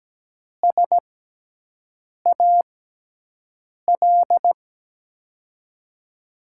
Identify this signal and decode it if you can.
Morse code, heard as a keyed tone, about 17 wpm, spaced as for 6 wpm, 700 Hz. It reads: SAL
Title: USA Call Signs - 6 Characters - 6x Character Spacing - 17wpm